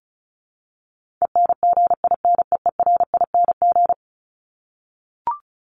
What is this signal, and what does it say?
Morse code, heard as a keyed tone, about 34 wpm, 700 Hz. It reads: ENGINEERING
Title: ING Suffix - Top 100 Words with Base Words 35wpm